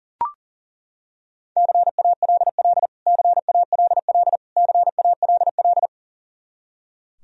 Morse code, heard as a keyed tone, about 40 wpm, 700 Hz. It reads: CALL CALL CALL
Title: Top 100 Words Repeated 40wpm